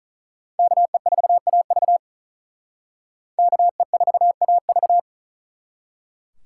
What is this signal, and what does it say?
Morse code, heard as a keyed tone, about 43 wpm, 700 Hz. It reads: KE4AV KE4AV
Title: Speed Racing - USA 2x2 Call Signs 35wpm